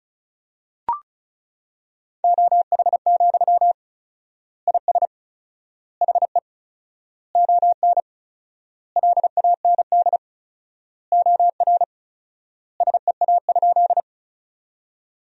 Morse code, heard as a keyed tone, about 36 wpm, 700 Hz. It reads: OH, IS HE ON LAND OR SEA?
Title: 4x Word-Spacing - Sentences from Top 500 Words - 35wpm